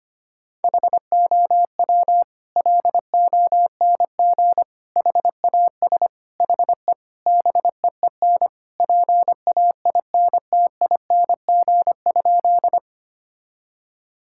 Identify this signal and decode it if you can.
Morse code, heard as a keyed tone, about 25 wpm, 700 Hz. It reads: HOW LONG HAS HE BEEN PAINTING?
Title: Sentences from Top 1000 Words - Review Part 6 - 25wpm